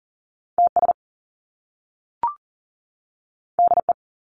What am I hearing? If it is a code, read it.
Morse code, heard as a keyed tone, about 40 wpm, 700 Hz. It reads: TS DE
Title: Top Two Letter Combinations in Words 40wpm